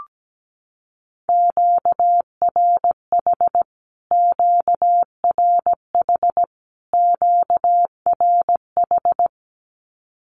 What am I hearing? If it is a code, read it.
Morse code, heard as a keyed tone, about 17 wpm, 700 Hz. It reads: QRH QRH QRH
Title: Q-Codes Repeated 3x 17wpm